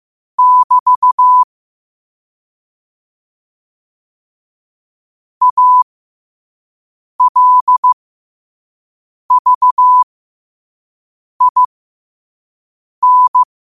Morse code, heard as a keyed tone, about 15 wpm, spaced as for 5 wpm, 1000 Hz. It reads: <BT> ALVIN
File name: qotdfiles / 5wpm